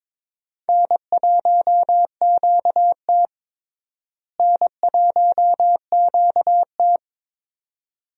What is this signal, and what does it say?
Morse code, heard as a keyed tone, about 22 wpm, 700 Hz. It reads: N1QT N1QT
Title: USA Call Signs - 4 Characters - Repeated 2x - 22wpm